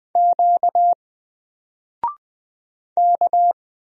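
Morse code, heard as a keyed tone, about 20 wpm, 700 Hz. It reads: Q K
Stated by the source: Single letter and number Morse drill at 20 wpm, from N4LCD.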